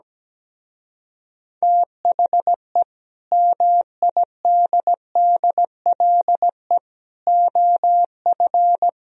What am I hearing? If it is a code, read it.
Morse code, heard as a keyed tone, about 17 wpm, 700 Hz. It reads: THE MIDDLE OF
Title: Speed-Racing - Sets of 4 Words 17wpm